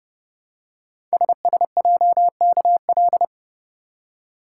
Morse code, heard as a keyed tone, about 30 wpm, 700 Hz. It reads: SSJKL
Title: Code Groups - 5 Letters - 30wpm